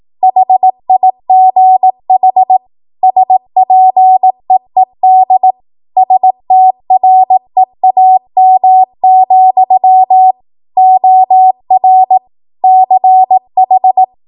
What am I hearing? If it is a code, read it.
Morse code, heard as a keyed tone, about 18 wpm, 750 Hz. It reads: HIGH SPEED STREAM, OR CH
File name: Morse / 18WPM